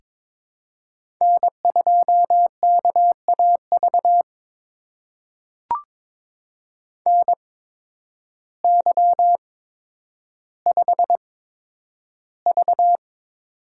Morse code, heard as a keyed tone, about 22 wpm, spaced as for 10 wpm, 700 Hz. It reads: N2KAV NY5V
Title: USA Call Signs - 8x Character Spacing - 22wpm